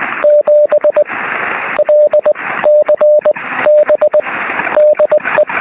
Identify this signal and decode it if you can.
Morse code, heard as a keyed tone, about 20 wpm, spaced as for 13 wpm, 600 Hz, with radio noise. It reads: 7LCB DE
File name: sm7lcb / 04-dl0vv 23cm